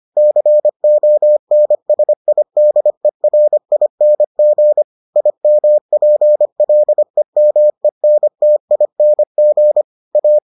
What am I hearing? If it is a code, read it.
Morse code, heard as a keyed tone, about 25 wpm, 600 Hz. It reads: CONSIDERING IMPLEMENTING A